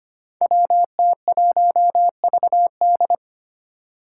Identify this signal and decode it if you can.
Morse code, heard as a keyed tone, about 25 wpm, 700 Hz. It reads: WT1VD
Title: Five Challenging Letter-Number Combinations 25wpm